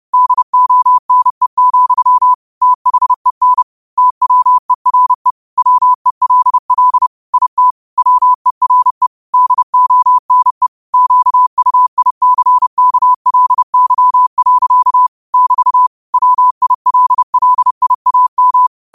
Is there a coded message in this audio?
Yes: Morse code from 0.1 to 18.7 s, heard as a keyed tone, about 30 wpm, 1000 Hz. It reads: DONE, THEN TWERE WELL IT WERE DONE QUICKLY. <BT> WILLIAM